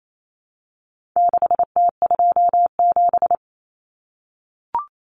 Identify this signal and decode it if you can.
Morse code, heard as a keyed tone, about 28 wpm, 700 Hz. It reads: 6T27
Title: Common Cut Numbers - 4-Digit 28wpm